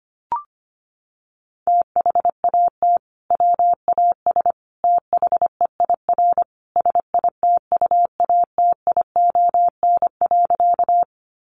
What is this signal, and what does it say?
Morse code, heard as a keyed tone, about 25 wpm, 700 Hz. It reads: THAT WAS THEIR SITUATION.
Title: Sentences from Top 1200 Words 25wpm